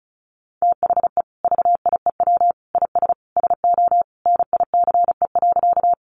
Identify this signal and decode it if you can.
Morse code, heard as a keyed tone, about 34 wpm, 700 Hz. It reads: THE VIEW IS SO NICE.
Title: Additional Easy Phrases 35wpm